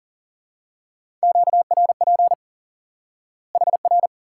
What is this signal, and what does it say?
Morse code, heard as a keyed tone, about 40 wpm, 700 Hz. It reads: QRP HR
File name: IQE Course 3 - Lesson 89 - YAGI - Two Elements - 40wpm